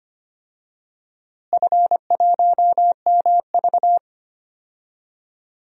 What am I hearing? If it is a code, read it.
Morse code, heard as a keyed tone, about 25 wpm, 700 Hz. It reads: F1MV
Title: DXCC-100 Call Signs - 4 Characters - Repeated 2x - 25wpm